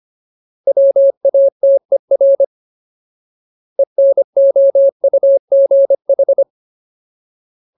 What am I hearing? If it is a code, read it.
Morse code, heard as a keyed tone, about 25 wpm, 550 Hz. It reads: WATER ENOUGH
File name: Ultimate Word List / 25WPM Ultimate